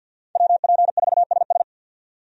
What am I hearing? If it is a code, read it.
Morse code, heard as a keyed tone, about 49 wpm, 700 Hz. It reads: WP4SS
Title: USA Call Signs - 5 Characters - 50wpm